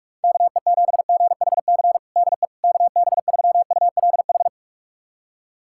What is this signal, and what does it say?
Morse code, heard as a keyed tone, about 46 wpm, 700 Hz. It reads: KE7GHC DE KB3ULH
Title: USA Callsign DE Callsign - 6 Characters - 45wpm